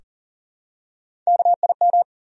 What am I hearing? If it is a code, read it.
Morse code, heard as a keyed tone, about 40 wpm, 700 Hz. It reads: KIM
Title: USA Common Names - Max 3 Letters - 40wpm